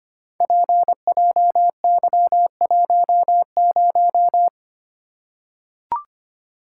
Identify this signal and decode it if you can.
Morse code, heard as a keyed tone, about 25 wpm, 700 Hz. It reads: PJY10
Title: Code Groups - 5 Letter:Numbers - 25wpm